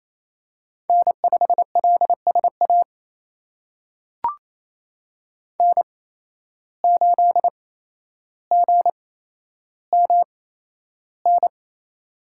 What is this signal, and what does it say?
Morse code, heard as a keyed tone, about 28 wpm, spaced as for 12 wpm, 700 Hz. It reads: N5LSA N8GMN
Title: USA Call Signs - 5 Characters - 8x Character Spacing - 28wpm